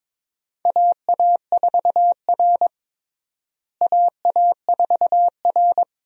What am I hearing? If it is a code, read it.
Morse code, heard as a keyed tone, about 22 wpm, 700 Hz. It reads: AA4R AA4R
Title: USA Call Signs - 4 Characters - Repeated 2x - 22wpm